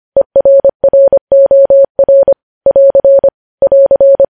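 Morse code, heard as a keyed tone, about 25 wpm, 550 Hz. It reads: ERROR <AR> <AR>
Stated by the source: Morse code news headlines at 25 words per minute.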